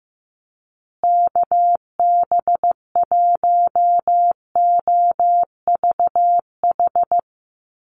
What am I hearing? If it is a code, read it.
Morse code, heard as a keyed tone, about 15 wpm, 700 Hz. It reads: KB1OVH
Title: USA Call Signs Repeated 2x 15wpm